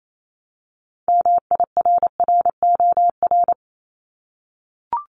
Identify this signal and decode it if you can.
Morse code, heard as a keyed tone, about 28 wpm, 700 Hz. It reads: MIRROR
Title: Multi-letter Phonemes Repeated 3x 28wpm